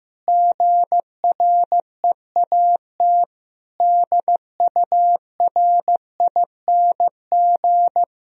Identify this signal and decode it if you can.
Morse code, heard as a keyed tone, about 15 wpm, 700 Hz. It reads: GREAT DURING
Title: Sentences from Top 1000 Words - Review Part 3 - 15wpm